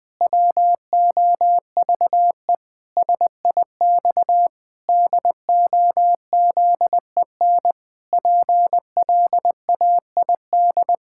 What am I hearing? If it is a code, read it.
Morse code, heard as a keyed tone, about 21 wpm, 700 Hz. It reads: WOVE SIX DOZEN PLAID
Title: Pangram Challenge v2 - 20wpm